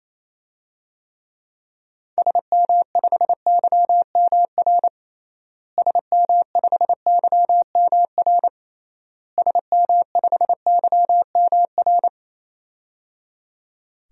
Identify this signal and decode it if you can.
Morse code, heard as a keyed tone, about 28 wpm, 700 Hz. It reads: SM5YMR SM5YMR SM5YMR
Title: DXCC-100 Call Signs - 6 Characters - Repeated 3x - 28wpm